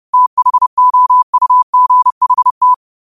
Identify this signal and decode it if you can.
Morse code, heard as a keyed tone, about 30 wpm, 1000 Hz. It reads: THOUGHT
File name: qotdfiles / 30wpm